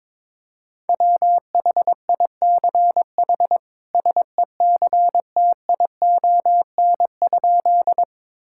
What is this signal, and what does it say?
Morse code, heard as a keyed tone, about 22 wpm, 700 Hz. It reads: WHICH SECTION?
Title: Sentences from Top 1000 Words - Review Part 5 - 22wpm